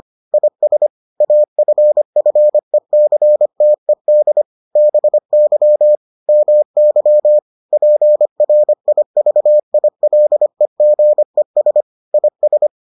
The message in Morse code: IS AFFECTED BY MY PRIVILEGES IS

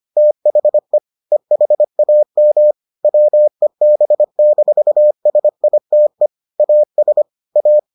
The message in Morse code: THE EHAM WEB-SITE AS A

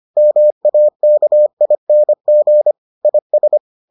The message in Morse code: MAKING IS

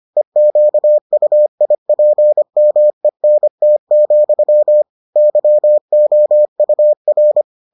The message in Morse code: EQUIPMENT, YOUR

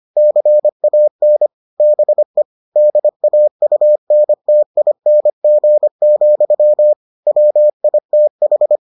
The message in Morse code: CAN BE DAUNTING, WITH

